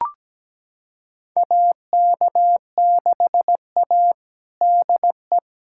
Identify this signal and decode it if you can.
Morse code, heard as a keyed tone, about 17 wpm, 700 Hz. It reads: AK6A DE
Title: USA Callsign DE Callsign - 4 Characters - 17wpm